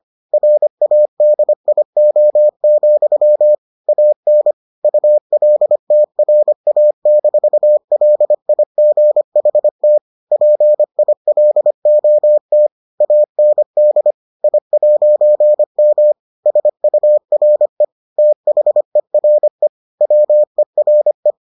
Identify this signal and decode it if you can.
Morse code, heard as a keyed tone, about 25 wpm, 600 Hz. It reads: RADIO, AN ULTRA-LIGHT PILOT AND I'M SURE THERE WERE